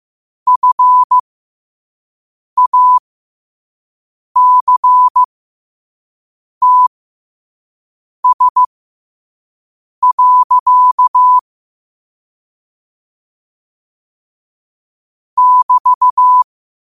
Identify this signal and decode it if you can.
Morse code, heard as a keyed tone, about 15 wpm, spaced as for 5 wpm, 1000 Hz. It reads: FACTS. <BT>
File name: qotdfiles / 5wpm